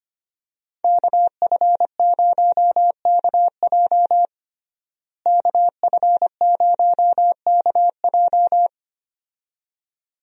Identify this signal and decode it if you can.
Morse code, heard as a keyed tone, about 25 wpm, 700 Hz. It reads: KF0KJ KF0KJ
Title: USA Call Signs - 5 Characters - Repeated 2x - 25wpm